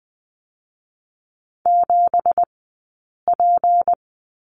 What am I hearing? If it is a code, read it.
Morse code, heard as a keyed tone, about 20 wpm, 700 Hz. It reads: 7 P
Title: 165 Lesson 27d - Number 7 - Two Characters 20wpm